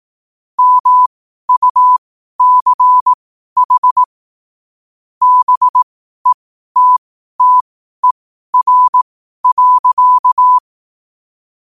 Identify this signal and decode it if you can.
Morse code, heard as a keyed tone, about 18 wpm, spaced as for 12 wpm, 1000 Hz. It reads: MUCH BETTER.
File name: qotdfiles / 12wpm